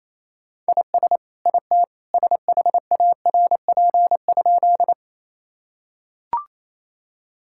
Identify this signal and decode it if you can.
Morse code, heard as a keyed tone, about 28 wpm, 700 Hz. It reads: IS IT SHARP?
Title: Sentences from Top 1600 Words 28wpm